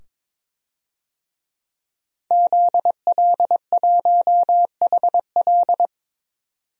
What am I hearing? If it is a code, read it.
Morse code, heard as a keyed tone, about 22 wpm, 700 Hz. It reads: ZL1HL
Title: DXCC-100 Call Signs - 5 Characters - 22wpm